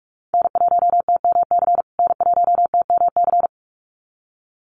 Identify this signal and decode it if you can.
Morse code, heard as a keyed tone, about 44 wpm, 700 Hz. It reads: N1TMC N1TMC
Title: USA Call Signs Repeated 2x 45wpm